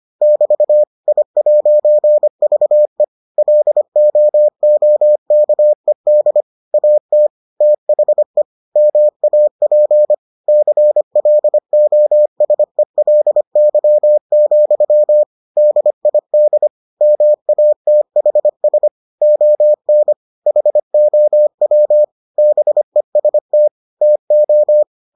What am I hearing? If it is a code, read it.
Morse code, heard as a keyed tone, about 25 wpm, 600 Hz. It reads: <BT> I'VE LOOKED AT THE MAP CLOSELY, DID MATHS ON HOW BEST TO